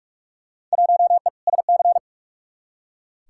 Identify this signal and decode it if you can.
Morse code, heard as a keyed tone, about 41 wpm, 700 Hz. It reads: 1E SC
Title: Field Day Exchanges - 45wpm